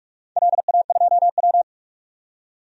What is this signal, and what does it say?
Morse code, heard as a keyed tone, about 44 wpm, 700 Hz. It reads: LA2W
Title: DXCC-100 Call Signs - 4 Characters - 45wpm